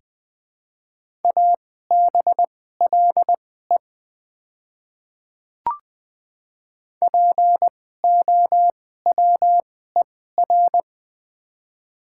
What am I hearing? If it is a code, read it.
Morse code, heard as a keyed tone, about 20 wpm, spaced as for 14 wpm, 700 Hz. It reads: ABLE POWER
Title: 2x Character Spacing - Top 500 Words - 20wpm